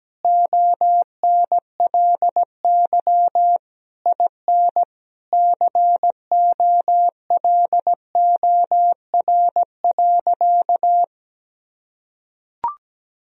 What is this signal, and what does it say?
Morse code, heard as a keyed tone, about 17 wpm, 700 Hz. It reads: ONLY IN COLOR.
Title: Sentences from Top 1000 Words - Review Part 2 - 17wpm